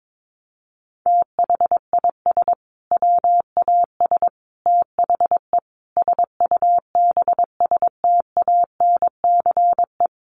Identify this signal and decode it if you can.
Morse code, heard as a keyed tone, about 22 wpm, 700 Hz. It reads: THIS WAS THE SUBSTANCE
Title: Sentences from Top 1500 Words 22wpm